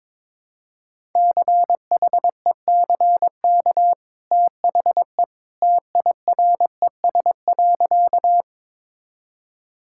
Morse code, heard as a keyed tone, about 22 wpm, 700 Hz. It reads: CHECK THE TIRES.
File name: practice / 22wpm / Easy Sentences with Hints - 22wpm